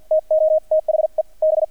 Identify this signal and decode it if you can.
Morse code, heard as a keyed tone, about 50 wpm, spaced as for 42 wpm, 630 Hz, with radio noise. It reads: TO THE B
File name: archived / CRSnet